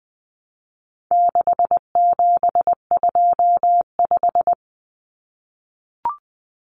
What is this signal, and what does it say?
Morse code, heard as a keyed tone, about 20 wpm, 700 Hz. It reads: 6725
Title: Numbers - 4-Digit 20wpm